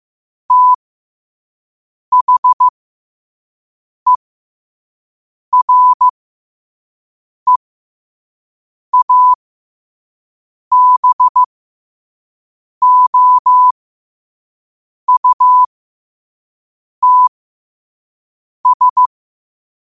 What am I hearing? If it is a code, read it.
Morse code, heard as a keyed tone, about 15 wpm, spaced as for 5 wpm, 1000 Hz. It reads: THEREABOUTS
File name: qotdfiles / 5wpm